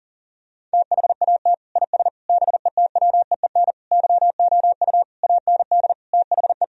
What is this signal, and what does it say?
Morse code, heard as a keyed tone, about 40 wpm, 700 Hz. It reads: THAT IS BETWEEN YOU AND THE
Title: Sentences from Top 1000 Words - Review Part 4 - 40wpm